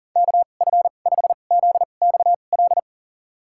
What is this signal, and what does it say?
Morse code, heard as a keyed tone, about 40 wpm, spaced as for 29 wpm, 700 Hz. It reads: KF5ZXL
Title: USA Call Signs - 2x Character Spacing - 40wpm